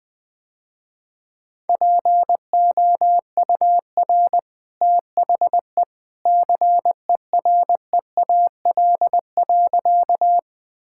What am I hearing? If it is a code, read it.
Morse code, heard as a keyed tone, about 20 wpm, 700 Hz. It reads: POUR THE CEREAL.
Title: Easy Sentences with Hints - 20wpm